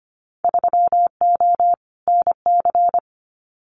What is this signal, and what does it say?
Morse code, heard as a keyed tone, about 25 wpm, 700 Hz. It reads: 3O NC
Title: Winter Field Day Exchanges 25wpm